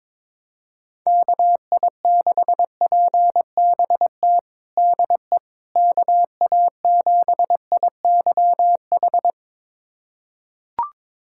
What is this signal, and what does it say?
Morse code, heard as a keyed tone, about 22 wpm, 700 Hz. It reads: KI6PBT DE KA7IYH
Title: USA Callsign DE Callsign - 6 Characters - 22wpm